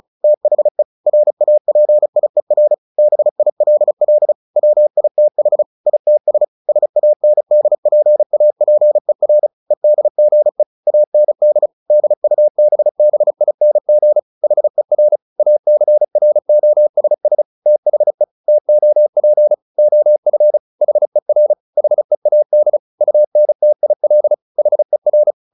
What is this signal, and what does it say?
Morse code, heard as a keyed tone, about 34 wpm, 600 Hz. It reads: THE RAPIER BILL WITH ITS SANDPAPER EDGE AND DUBBING HER ACROSS THE TOP OF HER HEAD UNTIL HER
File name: the old man and the sea / the old man and the sea 1 35wpm